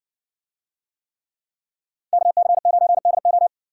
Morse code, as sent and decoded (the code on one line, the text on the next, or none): -.- -.-. ----. -.. ---
KC9DO